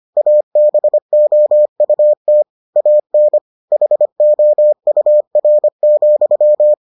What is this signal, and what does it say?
Morse code, heard as a keyed tone, about 25 wpm, 600 Hz. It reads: ABOUT AN HOUR,